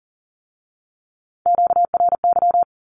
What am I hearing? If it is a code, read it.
Morse code, heard as a keyed tone, about 38 wpm, 700 Hz. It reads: QRY
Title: Q-Codes with Spoken Meaning - 40wpm